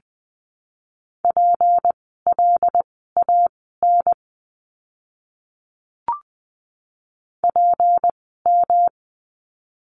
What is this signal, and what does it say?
Morse code, heard as a keyed tone, about 20 wpm, spaced as for 15 wpm, 700 Hz. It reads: PLAN PM
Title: 153 Lesson 25f - Letter p - Words 20wpm